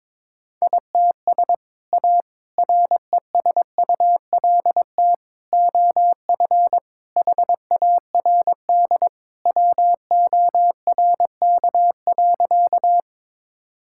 Morse code, heard as a keyed tone, about 22 wpm, 700 Hz. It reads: ITS A RESULT OF HARD WORK.
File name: practice / 22wpm / Sentences with 3-Word Phrases - 22wpm